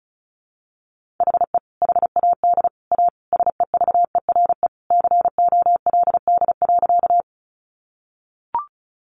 Morse code, heard as a keyed tone, about 34 wpm, 700 Hz. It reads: HE HAD A SEVERE COLD.
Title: Sentences from Top 1400 Words 35wpm